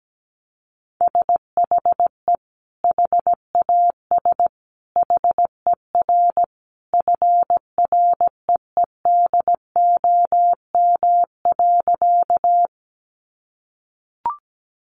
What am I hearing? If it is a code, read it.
Morse code, heard as a keyed tone, about 17 wpm, 700 Hz. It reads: SHE HAS HER FREEDOM.
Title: Sentences from Top 800 Words 17wpm